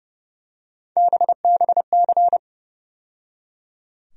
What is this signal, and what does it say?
Morse code, heard as a keyed tone, about 30 wpm, 700 Hz. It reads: BBC
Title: USA Call Sign Postfixes - 3 Letters 30wpm